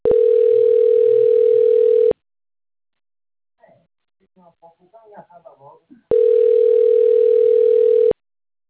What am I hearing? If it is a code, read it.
Ringback tone (North America), 440 + 480 Hz, with speech.
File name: rg-1600-unknown- - - .24806